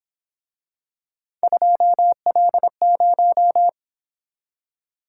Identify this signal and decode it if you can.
Morse code, heard as a keyed tone, about 26 wpm, 700 Hz. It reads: 2L0
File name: Speed-Racing - Three Letter-Number Combinations - 22wpm